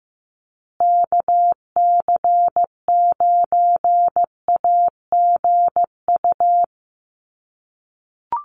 Morse code, heard as a keyed tone, about 15 wpm, 700 Hz. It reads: KC9AGU
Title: USA Call Signs Repeated 2x 15wpm